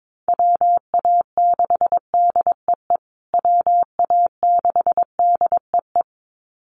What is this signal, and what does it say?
Morse code, heard as a keyed tone, about 22 wpm, 700 Hz. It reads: WA6DEE WA6DEE
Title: USA Call Signs Repeated 2x 22wpm